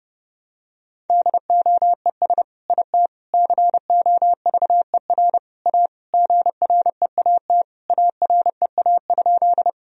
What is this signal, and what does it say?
Morse code, heard as a keyed tone, about 30 wpm, 700 Hz. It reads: DOES IT COVER A GREAT AREA?
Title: Sentences from Top 1000 Words - Review Part 2 - 30wpm